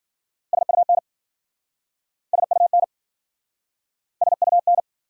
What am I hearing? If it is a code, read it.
Morse code, heard as a keyed tone, about 52 wpm, 700 Hz. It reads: SUN SUN SUN